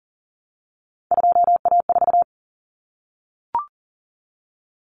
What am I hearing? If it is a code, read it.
Morse code, heard as a keyed tone, about 40 wpm, 700 Hz. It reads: 2A4
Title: Common Cut Numbers - 3-Digit 40wpm